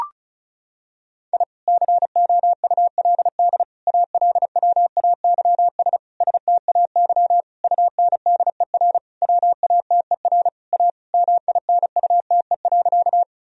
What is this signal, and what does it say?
Morse code, heard as a keyed tone, about 36 wpm, 700 Hz. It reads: I COULD ALWAYS STAY UNDER WATER A MINUTE.
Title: Sentences from Top 1000 Words - Review Part 1 - 35wpm